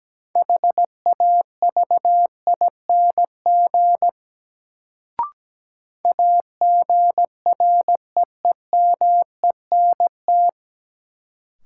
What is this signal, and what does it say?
Morse code, heard as a keyed tone, about 17 wpm, 700 Hz. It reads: HAVING AGREEMENT
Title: Sets of 5 Words Encore - New Word Drill - 17wpm